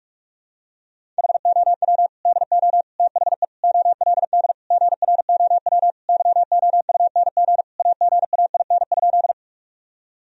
Morse code, heard as a keyed tone, about 41 wpm, 700 Hz. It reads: HOW DO THE OLD GROW YOUNG AGAIN?
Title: Sentences from Top 1000 Words - Review Part 1 - 45wpm